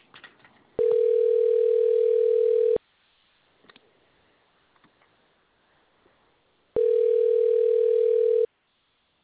Ringback tone (North America), 440 + 480 Hz.